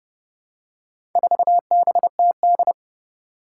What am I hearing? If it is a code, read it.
Morse code, heard as a keyed tone, about 30 wpm, 700 Hz. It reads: 4BTD
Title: Four Challenging Letter-Number Combinations 30wpm